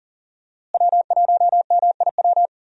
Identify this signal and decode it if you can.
Morse code, heard as a keyed tone, about 37 wpm, 700 Hz. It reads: W1MIW
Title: USA Call Signs - 5 Characters - Repeated 3x - 40wpm